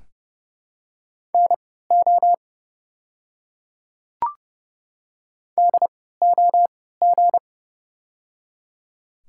Morse code, heard as a keyed tone, about 30 wpm, spaced as for 17 wpm, 700 Hz. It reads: NO DOG
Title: 3x Character Spacing - Top 500 Words - 30wpm